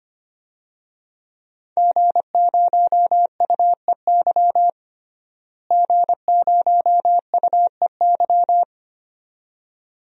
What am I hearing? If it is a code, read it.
Morse code, heard as a keyed tone, about 25 wpm, 700 Hz. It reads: G0UEY G0UEY
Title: DXCC-100 Call Signs - 5 Characters - Repeated 2x - 25wpm